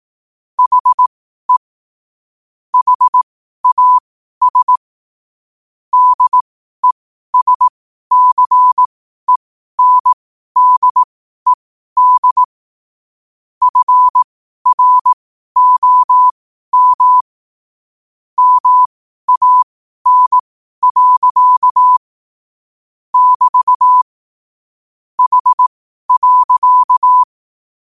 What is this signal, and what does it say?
Morse code, heard as a keyed tone, about 18 wpm, spaced as for 12 wpm, 1000 Hz. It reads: HE HAS DESCENDED FROM MAN. <BT> H.
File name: qotdfiles / 12wpm